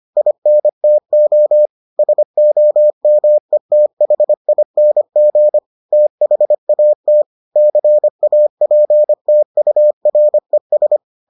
Morse code, heard as a keyed tone, about 25 wpm, 600 Hz. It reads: INTO SOMETHING THAT CAPTURES